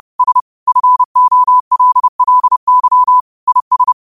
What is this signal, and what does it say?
Morse code, heard as a keyed tone, about 30 wpm, 1000 Hz. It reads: S FOLLY IS
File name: qotdfiles / 30wpm